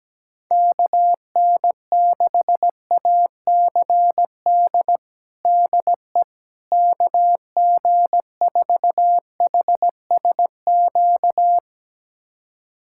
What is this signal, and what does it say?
Morse code, heard as a keyed tone, about 17 wpm, 700 Hz. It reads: KN6ACD DE KG4HSQ
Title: USA Callsign DE Callsign 17wpm